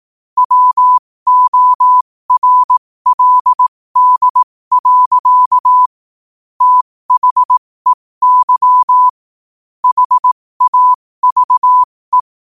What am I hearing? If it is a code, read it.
Morse code, heard as a keyed tone, about 18 wpm, spaced as for 15 wpm, 1000 Hz. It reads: WORLD. THEY HAVE
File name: qotdfiles / 15wpm